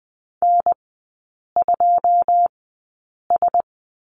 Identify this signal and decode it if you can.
Morse code, heard as a keyed tone, about 20 wpm, spaced as for 8 wpm, 700 Hz. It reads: N2S
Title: 114 Lesson 19e - Letter w - Three Characters 20wpm